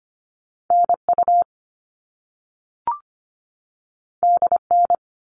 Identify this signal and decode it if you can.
Morse code, heard as a keyed tone, about 25 wpm, 700 Hz. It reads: NU DN
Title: All Cut Numbers - 2-Digit 25wpm